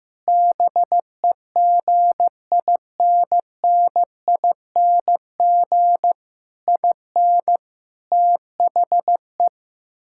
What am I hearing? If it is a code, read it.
Morse code, heard as a keyed tone, about 15 wpm, 700 Hz. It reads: BEGINNING IN THE
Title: Sentences from Top 1000 Words - Review Part 6 - 15wpm